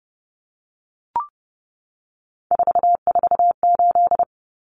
Morse code, heard as a keyed tone, about 30 wpm, 700 Hz. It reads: 448
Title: Common Cut Numbers - 3-Digit 30wpm